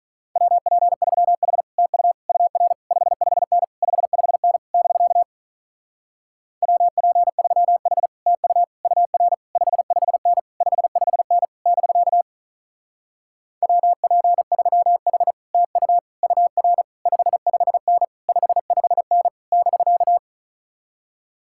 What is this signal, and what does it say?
Morse code, heard as a keyed tone, about 46 wpm, 700 Hz. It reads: WP3H TU UR 55N 55N <BK> WP3H TU UR 55N 55N <BK> WP3H TU UR 55N 55N <BK>